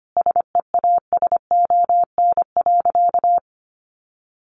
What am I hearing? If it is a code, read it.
Morse code, heard as a keyed tone, about 25 wpm, 700 Hz. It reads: SEASON.
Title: Sentences from Top 1400 Words 25wpm